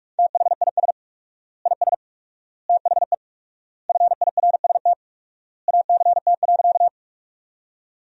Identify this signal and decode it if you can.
Morse code, heard as a keyed tone, about 47 wpm, 700 Hz. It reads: THIS IS THE FIRST ACT.
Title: 4x Word-Spacing - Sentences from Top 200 Words - 45wpm